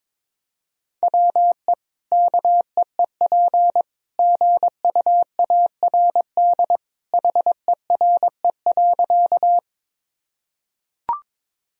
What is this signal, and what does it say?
Morse code, heard as a keyed tone, about 22 wpm, 700 Hz. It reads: WE KEEP GUARD HERE.
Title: Sentences from Top 1000 Words - Review Part 4 - 22wpm